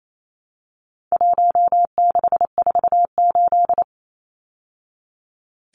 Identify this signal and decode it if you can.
Morse code, heard as a keyed tone, about 28 wpm, 700 Hz. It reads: 1648
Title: Numbers - 4-Digit 28wpm